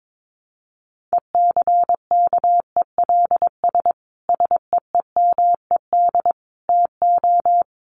Morse code, heard as a keyed tone, about 22 wpm, 700 Hz. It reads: ECKELS SEEMED TO
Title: A Sound Of Thunder 22wpm